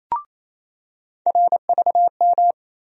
Morse code, heard as a keyed tone, about 28 wpm, 700 Hz. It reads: RVM
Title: USA Call Sign Postfixes - 3 Letters 28wpm